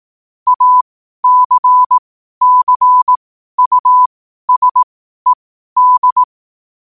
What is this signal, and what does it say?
Morse code, heard as a keyed tone, about 18 wpm, spaced as for 12 wpm, 1000 Hz. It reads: ACCUSED
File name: qotdfiles / 12wpm